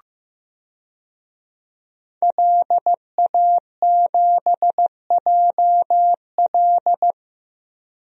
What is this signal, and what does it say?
Morse code, heard as a keyed tone, about 15 wpm, 700 Hz. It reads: LA7JL
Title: DXCC-100 Call Signs - 5 Characters - Repeated 3x - 15wpm